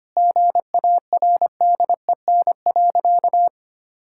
Morse code, heard as a keyed tone, about 25 wpm, 700 Hz. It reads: GARDEN.
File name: practice / 25wpm / Sentences with 3-Word Phrases - 25wpm